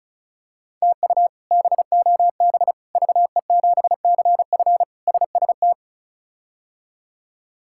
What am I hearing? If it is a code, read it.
Morse code, heard as a keyed tone, about 36 wpm, 700 Hz. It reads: TU BOB VE7CF SST
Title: K1USN SST QSO Practice - 35wpm